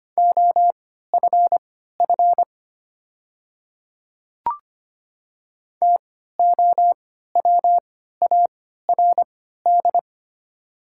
Morse code, heard as a keyed tone, about 25 wpm, spaced as for 14 wpm, 700 Hz. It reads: OFF TOWARD
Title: 3x Character Spacing - Top 500 Words - 25wpm